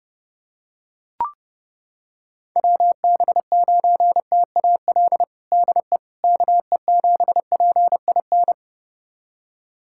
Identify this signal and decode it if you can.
Morse code, heard as a keyed tone, about 32 wpm, 700 Hz. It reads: WB9TAL DE KE7PIN
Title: USA Callsign DE Callsign - 6 Characters - 30wpm